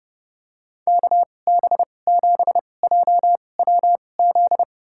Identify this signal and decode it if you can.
Morse code, heard as a keyed tone, about 31 wpm, spaced as for 22 wpm, 700 Hz. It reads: KB7JWZ
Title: USA Call Signs - 6 Characters - 2x Character Spacing - 30wpm